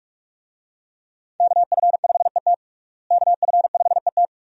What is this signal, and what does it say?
Morse code, heard as a keyed tone, about 46 wpm, 700 Hz. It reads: KF5ET KF5ET
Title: USA Call Signs - 5 Characters - Repeated 2x - 45wpm